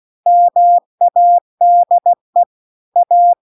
Morse code, heard as a keyed tone, about 16 wpm, 700 Hz. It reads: MADE A